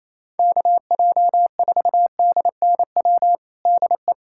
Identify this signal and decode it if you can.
Morse code, heard as a keyed tone, about 27 wpm, 700 Hz. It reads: KJ4DNW DE
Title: USA Callsign DE Callsign - 6 Characters - 28wpm